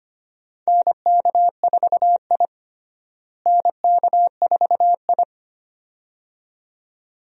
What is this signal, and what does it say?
Morse code, heard as a keyed tone, about 25 wpm, 700 Hz. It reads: NK4I NK4I
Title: USA Call Signs - 4 Characters - Repeated 2x - 25wpm